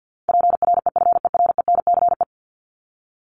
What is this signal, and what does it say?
Morse code, heard as a keyed tone, about 50 wpm, 700 Hz. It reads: PREFERENCE
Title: Top 1251-1500 Words 50wpm